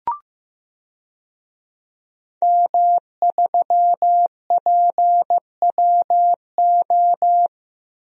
Morse code, heard as a keyed tone, about 15 wpm, 700 Hz. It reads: M3PWO
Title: DXCC-100 Call Signs - 5 Characters - 15wpm